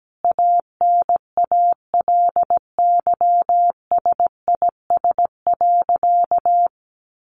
Morse code, heard as a keyed tone, about 17 wpm, 700 Hz. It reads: ANALYSIS.
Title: Sentences from Top 1400 Words 17wpm